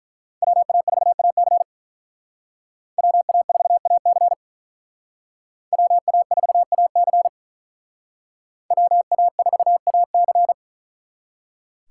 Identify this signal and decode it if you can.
Morse code, heard as a keyed tone, about 42 wpm, 700 Hz. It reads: WA4AC WA4AC WA4AC WA4AC
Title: Speed Racing - USA 2x2 Call Signs 35wpm